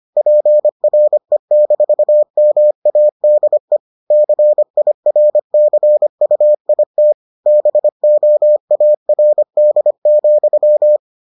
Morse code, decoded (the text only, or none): PRE-MADE CIRCUIT BOARD,